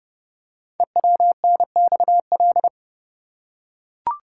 Morse code, heard as a keyed tone, about 30 wpm, 700 Hz. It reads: EWNXL
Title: Code Groups - 5 Letters - 30wpm